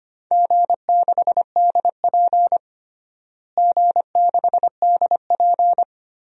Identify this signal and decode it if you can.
Morse code, heard as a keyed tone, about 25 wpm, 700 Hz. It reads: G6DP G6DP
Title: DXCC-100 Call Signs - 4 Characters - Repeated 3x - 25wpm